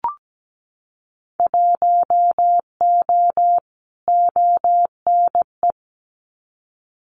Morse code, heard as a keyed tone, about 17 wpm, 700 Hz. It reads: 1O ONE
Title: Winter Field Day Exchanges 17wpm